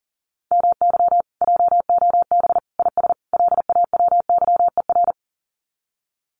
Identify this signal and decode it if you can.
Morse code, heard as a keyed tone, about 40 wpm, 700 Hz. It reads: MY JOB IS LAWYER
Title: Common QSO Phrases Encore 40wpm